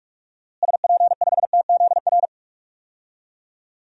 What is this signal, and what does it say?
Morse code, heard as a keyed tone, about 46 wpm, 700 Hz. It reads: SP5TZR
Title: DXCC-100 Call Signs - 6 Characters - Repeated 3x - 45wpm